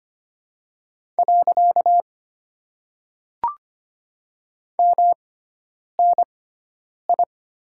Morse code, heard as a keyed tone, about 25 wpm, 700 Hz. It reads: . M N I
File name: Top 90 QSO Elements - 6x Character Spacing - 25wpm